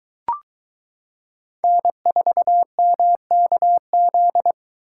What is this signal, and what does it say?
Morse code, heard as a keyed tone, about 22 wpm, 700 Hz. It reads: N4MKZ